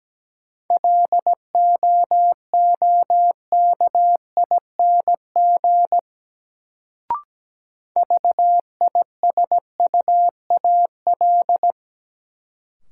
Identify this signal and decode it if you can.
Morse code, heard as a keyed tone, about 17 wpm, 700 Hz. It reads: LOOKING VISUAL
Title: Sentences from Top 1000 Words - Review Part 6 - New Word Drill - 17wpm